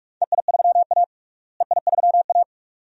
Morse code, heard as a keyed tone, about 45 wpm, 700 Hz. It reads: EI3A EI3A
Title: DXCC-100 Call Signs - 4 Characters - Repeated 2x - 45wpm